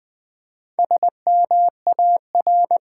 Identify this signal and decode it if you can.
Morse code, heard as a keyed tone, about 20 wpm, 700 Hz. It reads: SMAR
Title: Build A Word - 4 Letters At A Time - 10-Letter Words - 20wpm